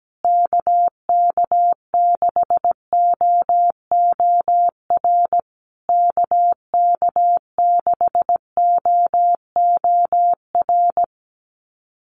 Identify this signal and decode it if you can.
Morse code, heard as a keyed tone, about 17 wpm, 700 Hz. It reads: KK6OOR KK6OOR
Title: USA Call Signs Repeated 3x 17wpm